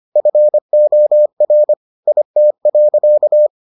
Morse code, heard as a keyed tone, about 25 wpm, 600 Hz. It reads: FOR IT.